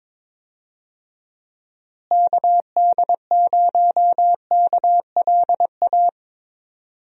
Morse code, heard as a keyed tone, about 22 wpm, 700 Hz. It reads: KD0KLA
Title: USA Call Signs - 6 Characters - Repeated 3x - 22wpm